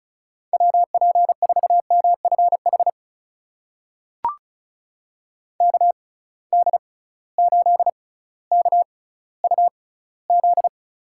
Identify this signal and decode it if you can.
Morse code, heard as a keyed tone, about 36 wpm, spaced as for 18 wpm, 700 Hz. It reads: WP4MFH KD8KUZ
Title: USA Call Signs - 6 Characters - 6x Character Spacing - 35wpm